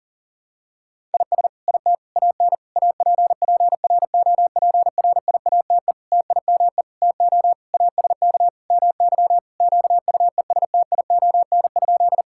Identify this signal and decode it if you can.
Morse code, heard as a keyed tone, about 40 wpm, 700 Hz. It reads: IS IT AN APPROPRIATE TIME TO ASK MY QUESTION?
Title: Sentences from Top 1000 Words - Review Part 5 - 40wpm